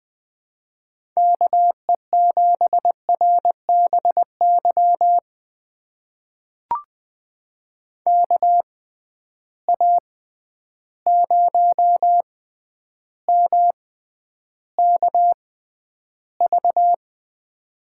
Morse code, heard as a keyed tone, about 20 wpm, spaced as for 10 wpm, 700 Hz. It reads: KE7RBY KA0MKV
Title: USA Call Signs - 6 Characters - 6x Character Spacing - 20wpm